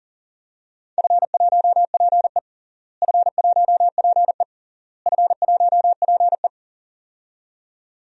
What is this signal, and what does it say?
Morse code, heard as a keyed tone, about 40 wpm, 700 Hz. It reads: F1PE F1PE F1PE
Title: DXCC-100 Call Signs - 4 Characters - Repeated 3x - 40wpm